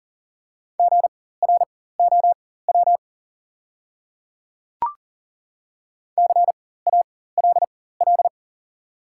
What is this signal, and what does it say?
Morse code, heard as a keyed tone, about 40 wpm, spaced as for 19 wpm, 690 Hz. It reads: GROW CALL
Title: 4x Character Spacing - Top 300 Words - 40wpm